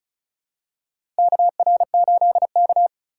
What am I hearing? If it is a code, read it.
Morse code, heard as a keyed tone, about 34 wpm, 700 Hz. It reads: KR8K
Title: USA Call Signs - 4 Characters - 4x Character Spacing - 35wpm